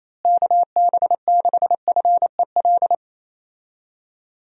Morse code, heard as a keyed tone, about 28 wpm, 700 Hz. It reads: KB6FEL
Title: USA Call Signs - 6 Characters - Repeated 2x - 28wpm